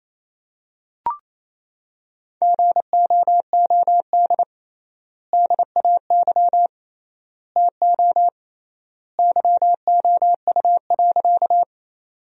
Morse code, heard as a keyed tone, about 28 wpm, 700 Hz. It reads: GOOD DAY TO YOU.
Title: 3x Word-Spacing - Sentences from Top 200 Words - 28wpm